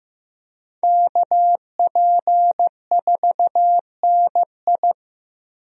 Morse code, heard as a keyed tone, about 15 wpm, 700 Hz. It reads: KP4NI
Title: USA Call Signs - 5 Characters - Repeated 3x - 15wpm